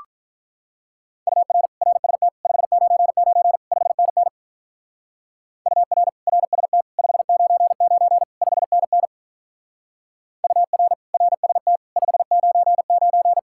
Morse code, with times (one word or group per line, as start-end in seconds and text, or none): 1.3-1.7 UR
1.8-2.3 RST
2.5-3.6 599
3.7-4.3 5NN
5.7-6.1 UR
6.3-6.8 RST
7.0-8.2 599
8.4-9.1 5NN
10.4-10.9 UR
11.1-11.8 RST
12.0-13.4 599